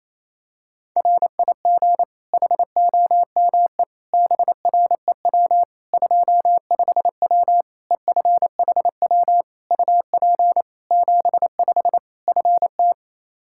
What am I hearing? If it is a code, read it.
Morse code, heard as a keyed tone, about 28 wpm, 700 Hz. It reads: RIG HOME BREW 25W EFHW UP 75 FT
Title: K1USN SST QSO Practice 28wpm